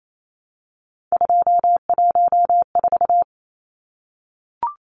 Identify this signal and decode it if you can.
Morse code, heard as a keyed tone, about 28 wpm, 700 Hz. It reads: 214
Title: Numbers - 3-Digit 28wpm